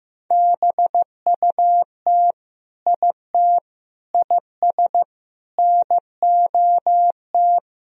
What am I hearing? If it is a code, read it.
Morse code, heard as a keyed tone, about 15 wpm, 700 Hz. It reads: BUT IT IS NOT
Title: Sentences from Top 1000 Words - Review Part 4 - 15wpm